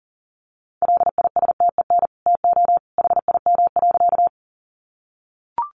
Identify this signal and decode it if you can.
Morse code, heard as a keyed tone, about 40 wpm, 700 Hz. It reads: LISTEN TO HIM.
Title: Sentences from Top 400 Words 40wpm